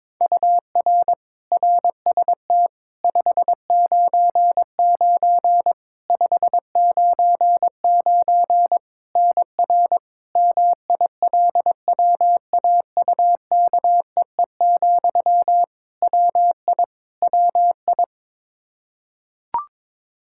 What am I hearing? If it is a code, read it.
Morse code, heard as a keyed tone, about 21 wpm, 700 Hz. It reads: UR RST 599 599 NR MILWAUKEE, WI WI
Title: Practice QSO - Exchange 1 - Beginning - 22wpm